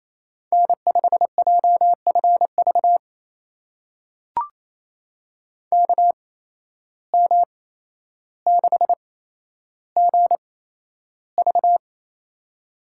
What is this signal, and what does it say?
Morse code, heard as a keyed tone, about 28 wpm, spaced as for 12 wpm, 700 Hz. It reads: N5JFV KM6GV
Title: USA Call Signs - 5 Characters - 8x Character Spacing - 28wpm